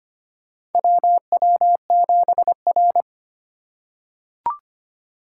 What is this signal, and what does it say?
Morse code, heard as a keyed tone, about 25 wpm, 700 Hz. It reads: WW7R
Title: USA Call Signs - 4 Characters - 3x Character Spacing - 25wpm